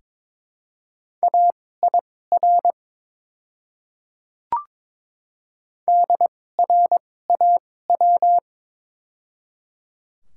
Morse code, heard as a keyed tone, about 22 wpm, spaced as for 16 wpm, 700 Hz. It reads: AIR DRAW
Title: 2x Character Spacing - Top 300 Words - 22wpm